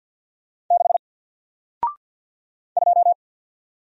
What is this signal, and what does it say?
Morse code, heard as a keyed tone, about 48 wpm, 700 Hz. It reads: 6 2
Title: Single Letter-Number Alt 50wpm